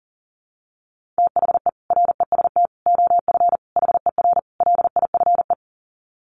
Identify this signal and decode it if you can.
Morse code, heard as a keyed tone, about 40 wpm, 700 Hz. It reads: THE REST OF HER LIFE
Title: Sets of 5 Words Encore 40wpm